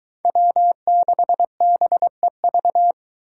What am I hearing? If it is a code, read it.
Morse code, heard as a keyed tone, about 23 wpm, 700 Hz. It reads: W6BEV